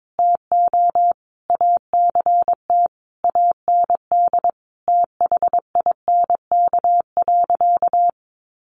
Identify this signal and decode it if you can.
Morse code, heard as a keyed tone, about 22 wpm, 700 Hz. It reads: TO ACT AND THINK.
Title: Sentences from Top 500 Words 22wpm